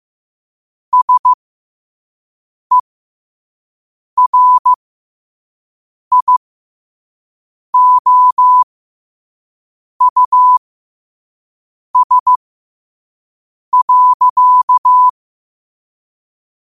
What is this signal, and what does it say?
Morse code, heard as a keyed tone, about 15 wpm, spaced as for 5 wpm, 1000 Hz. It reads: SERIOUS.